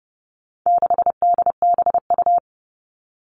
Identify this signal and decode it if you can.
Morse code, heard as a keyed tone, about 30 wpm, 700 Hz. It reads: 6DBU
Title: All Cut Numbers - 4-Digit 30wpm